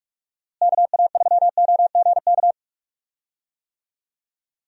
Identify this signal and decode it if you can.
Morse code, heard as a keyed tone, about 46 wpm, 700 Hz. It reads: KA3OGK
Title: USA Call Signs - 6 Characters - Repeated 3x - 45wpm